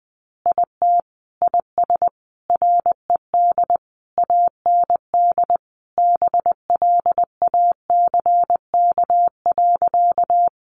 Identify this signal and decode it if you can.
Morse code, heard as a keyed tone, about 20 wpm, 700 Hz. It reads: IT IS RED AND BLACK.
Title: Easy Phrases 20wpm